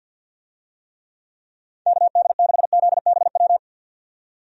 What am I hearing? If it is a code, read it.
Morse code, heard as a keyed tone, about 48 wpm, 700 Hz. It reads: KD6ZBW
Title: USA Call Signs - 6 Characters - 50wpm